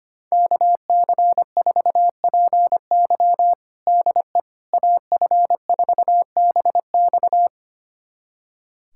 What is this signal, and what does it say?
Morse code, heard as a keyed tone, about 25 wpm, 700 Hz. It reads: KC4PY DE AF4BX
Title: USA Callsign DE Callsign - 5 Characters - 25wpm